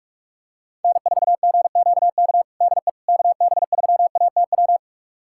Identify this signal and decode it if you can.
Morse code, heard as a keyed tone, about 46 wpm, 700 Hz. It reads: N4GQK DE KB3ATW
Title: USA Callsign DE Callsign 45wpm